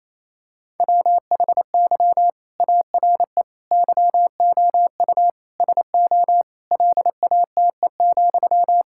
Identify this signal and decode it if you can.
Morse code, heard as a keyed tone, about 28 wpm, 700 Hz. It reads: WHY ARE YOU SO LATE,
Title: Sentences from Top 1000 Words - Review Part 6 - 28wpm